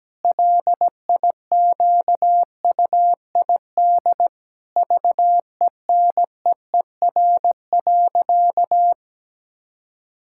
Morse code, heard as a keyed tone, about 17 wpm, 700 Hz. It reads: LIQUID VENEER.
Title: Pangram Challenge v2 - 17wpm